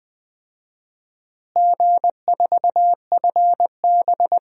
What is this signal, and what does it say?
Morse code, heard as a keyed tone, about 20 wpm, 700 Hz. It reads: G4FB
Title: DXCC-100 Call Signs - 4 Characters - Repeated 3x - 20wpm